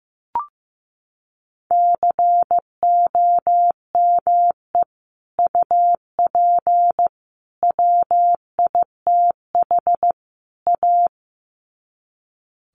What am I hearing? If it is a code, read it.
Morse code, heard as a keyed tone, about 15 wpm, 700 Hz. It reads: COME UP WITH A
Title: Sets of 4 Words 15wpm